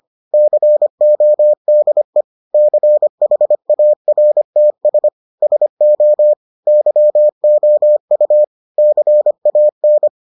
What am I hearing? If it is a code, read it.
Morse code, heard as a keyed tone, about 25 wpm, 600 Hz. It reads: CODE CHARTS SO YOU CAN